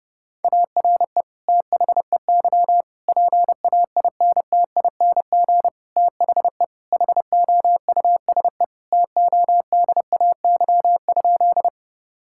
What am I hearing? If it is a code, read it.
Morse code, heard as a keyed tone, about 32 wpm, 700 Hz. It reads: ARE THEY PAINTING THE HOUSE TODAY?
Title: Sentences from Top 1000 Words - Review Part 6 - 30wpm